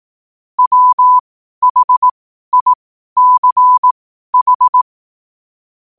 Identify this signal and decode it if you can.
Morse code, heard as a keyed tone, about 18 wpm, spaced as for 12 wpm, 1000 Hz. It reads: WHICH
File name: qotdfiles / 12wpm